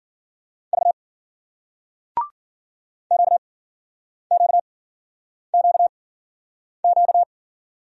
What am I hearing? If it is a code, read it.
Morse code, heard as a keyed tone, about 50 wpm, spaced as for 9 wpm, 700 Hz. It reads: V QQQQ